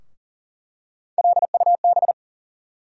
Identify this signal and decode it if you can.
Morse code, heard as a keyed tone, about 40 wpm, 700 Hz. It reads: LUB
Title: USA Call Sign Postfixes - 3 Letters 40wpm